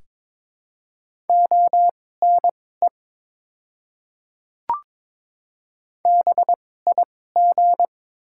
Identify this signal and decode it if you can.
Morse code, heard as a keyed tone, about 22 wpm, spaced as for 16 wpm, 700 Hz. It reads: ONE BIG
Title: 2x Character Spacing - Top 200 Words - 22wpm